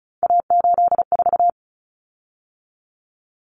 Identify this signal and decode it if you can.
Morse code, heard as a keyed tone, about 34 wpm, 700 Hz. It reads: A84
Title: Common Cut Numbers - 3-Digit 35wpm